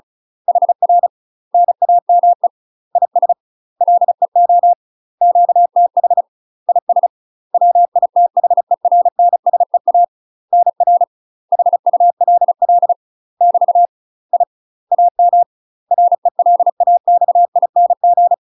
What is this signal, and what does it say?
Morse code, heard as a keyed tone, about 36 wpm, 700 Hz. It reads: HR NAME IS LEO QTH IS WITHERNSEA NR HULL <BT> I AM RELAXING
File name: UK MASTER QSOs TRAINING / 35WPM / UK MASTER QSOs 0009